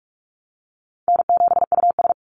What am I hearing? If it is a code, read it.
Morse code, heard as a keyed tone, about 44 wpm, 700 Hz. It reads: N7US
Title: Call Signs Repeated 2x - 45wpm